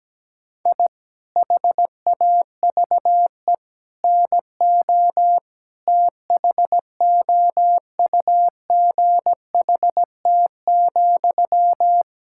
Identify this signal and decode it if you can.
Morse code, heard as a keyed tone, about 17 wpm, 700 Hz. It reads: I HAVE NO THOUGHT,
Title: Sentences from Top 1000 Words - Review Part 6 - 17wpm